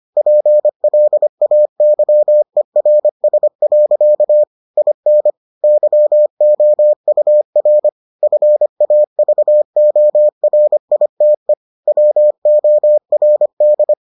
Morse code, heard as a keyed tone, about 25 wpm, 600 Hz. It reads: PLAYERS. IN YOUR FAVORITE WORD